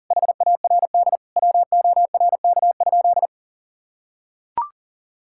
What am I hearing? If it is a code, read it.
Morse code, heard as a keyed tone, about 37 wpm, 700 Hz. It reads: HARD WORK?
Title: Sentences from Top 1000 Words - Review Part 6 - 40wpm